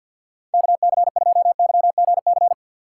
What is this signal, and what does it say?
Morse code, heard as a keyed tone, about 46 wpm, 690 Hz. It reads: KC2YGC
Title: USA Call Signs - 6 Characters - 3x Character Spacing - 50wpm